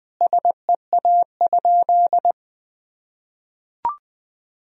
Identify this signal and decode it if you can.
Morse code, heard as a keyed tone, about 19 wpm, 700 Hz. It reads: SEA?
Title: Sentences from Top 1000 Words - Review Part 3 - 20wpm